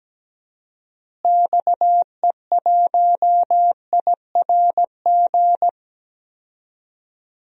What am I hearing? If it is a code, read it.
Morse code, heard as a keyed tone, about 17 wpm, 700 Hz. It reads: XE1IRG
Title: DXCC-100 Call Signs - 6 Characters - Repeated 2x - 17wpm